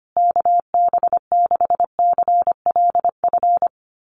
Morse code, heard as a keyed tone, about 25 wpm, 700 Hz. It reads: KB6CLF
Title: USA Call Signs Repeated 3x 25wpm